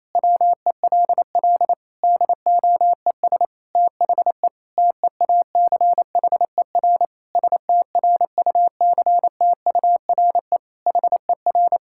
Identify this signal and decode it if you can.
Morse code, heard as a keyed tone, about 28 wpm, 700 Hz. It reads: WELL DOES THE TEACHER STRUCTURE HER
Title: Sentences from Top 1000 Words - Review Part 6 - 28wpm